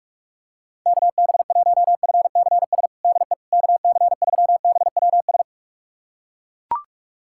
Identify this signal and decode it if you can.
Morse code, heard as a keyed tone, about 42 wpm, 690 Hz. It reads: KB1FCS DE KC3BWS
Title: USA Callsign DE Callsign 45wpm